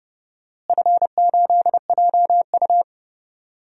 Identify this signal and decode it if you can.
Morse code, heard as a keyed tone, about 30 wpm, 700 Hz. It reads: F8JU
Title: DXCC-100 Call Signs - 4 Characters - Repeated 2x - 30wpm